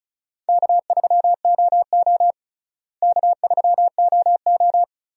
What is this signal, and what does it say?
Morse code, heard as a keyed tone, about 36 wpm, 700 Hz. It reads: K3OO K3OO
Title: USA Call Signs - 4 Characters - Repeated 3x - 35wpm